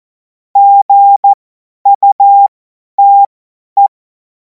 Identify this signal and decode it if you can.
Morse code, heard as a keyed tone, about 14 wpm, spaced as for 10 wpm, 800 Hz. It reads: GUTE
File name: Morsekurs / Witze / Witz 14. Juli 2026 50 F70